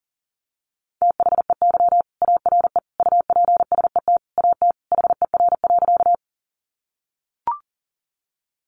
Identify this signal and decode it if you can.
Morse code, heard as a keyed tone, about 37 wpm, 700 Hz. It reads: THEY ARE UPSET AT HER.
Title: Sentences from Top 800 Words 40wpm